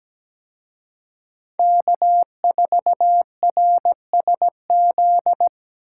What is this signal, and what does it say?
Morse code, heard as a keyed tone, about 20 wpm, 700 Hz. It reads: K4RSZ
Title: Speed Racing - USA 1x3 Call Signs 17wpm